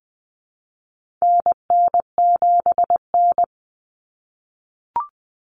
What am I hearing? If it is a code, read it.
Morse code, heard as a keyed tone, about 20 wpm, 700 Hz. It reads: NN7N
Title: Common Cut Numbers - 4-Digit 20wpm